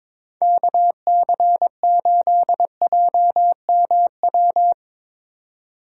Morse code, heard as a keyed tone, about 22 wpm, 700 Hz. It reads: KC8JMW
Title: Speed Racing - USA Call Signs 22wpm